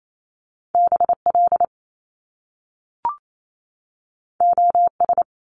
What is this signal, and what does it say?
Morse code, heard as a keyed tone, about 28 wpm, 700 Hz. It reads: BL OS
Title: Top Two Letter Combinations in Words 28wpm